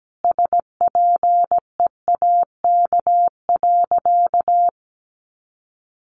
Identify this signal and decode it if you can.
Morse code, heard as a keyed tone, about 17 wpm, 700 Hz. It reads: SPEAK.
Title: Sentences from Top 1100 Words 17wpm